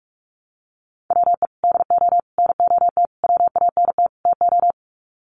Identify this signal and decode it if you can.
Morse code, heard as a keyed tone, about 44 wpm, 700 Hz. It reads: WE DO NOT WANT TO
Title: Sets of 5 Words 45wpm